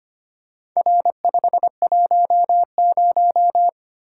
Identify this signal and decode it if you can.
Morse code, heard as a keyed tone, about 25 wpm, 700 Hz. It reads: R510
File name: practice / 25wpm / Four Letter-Number Combinations - 25wpm